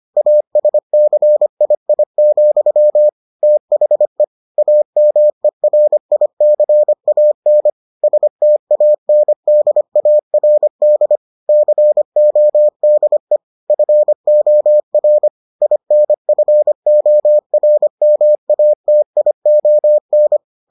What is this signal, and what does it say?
Morse code, heard as a keyed tone, about 25 wpm, 600 Hz. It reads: ASCII, THE AMERICAN STANDARD CODE FOR INFORMATION